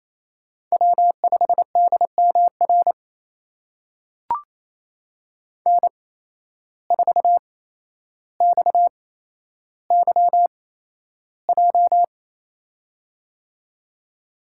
Morse code, heard as a keyed tone, about 27 wpm, spaced as for 12 wpm, 700 Hz. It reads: W5DMR N4XYJ
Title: USA Call Signs - 5 Characters - 8x Character Spacing - 28wpm